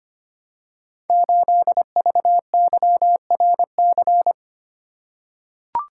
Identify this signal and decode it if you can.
Morse code, heard as a keyed tone, about 25 wpm, 700 Hz. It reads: 8VYRC
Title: Code Groups - 5 Letter:Numbers - 25wpm